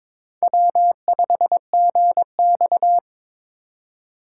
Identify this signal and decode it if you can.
Morse code, heard as a keyed tone, about 22 wpm, 700 Hz. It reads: W5GX
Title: USA Call Signs - 8x Character Spacing - 22wpm